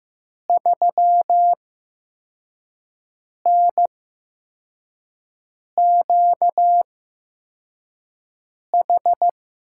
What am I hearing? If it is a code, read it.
Morse code, heard as a keyed tone, about 15 wpm, spaced as for 4 wpm, 700 Hz. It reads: 3NQH
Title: USA Call Signs - 6 Characters - 8x Character Spacing - 15wpm